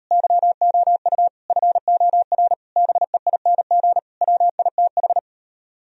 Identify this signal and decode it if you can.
Morse code, heard as a keyed tone, about 38 wpm, 700 Hz. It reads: YOU FOR BEING WITH